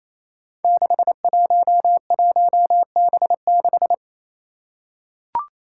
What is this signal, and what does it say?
Morse code, heard as a keyed tone, about 28 wpm, 700 Hz. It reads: 611B6
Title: Five Challenging Letter-Number Combinations 28wpm